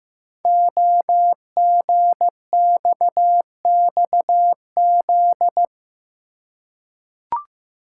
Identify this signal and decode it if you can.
Morse code, heard as a keyed tone, about 15 wpm, 700 Hz. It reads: OGXXZ
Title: Code Groups - 5 Letter:Numbers - 15wpm